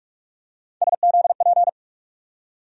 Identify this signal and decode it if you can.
Morse code, heard as a keyed tone, about 47 wpm, 700 Hz. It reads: SZP
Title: USA Call Sign Postfixes - 3 Letters 45wpm